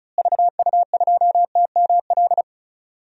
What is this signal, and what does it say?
Morse code, heard as a keyed tone, about 34 wpm, 700 Hz. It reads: VU2TML
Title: DXCC-100 Call Signs - 6 Characters - Repeated 2x - 35wpm